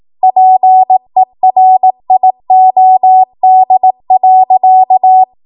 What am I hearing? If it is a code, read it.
Morse code, heard as a keyed tone, about 18 wpm, 750 Hz. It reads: PERIOD.
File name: Morse / 18WPM